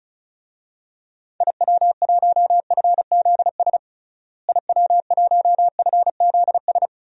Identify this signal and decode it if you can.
Morse code, heard as a keyed tone, about 34 wpm, 700 Hz. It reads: IW1FZS IW1FZS
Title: DXCC-100 Call Signs - 6 Characters - Repeated 2x - 35wpm